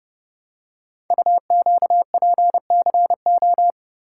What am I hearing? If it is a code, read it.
Morse code, heard as a keyed tone, about 30 wpm, 700 Hz. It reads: UQPCO
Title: Code Groups - 5 Letters - 30wpm